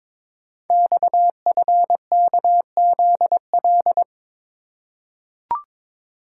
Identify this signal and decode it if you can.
Morse code, heard as a keyed tone, about 22 wpm, 700 Hz. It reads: XFKZL
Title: Code Groups - 5 Letters - 22wpm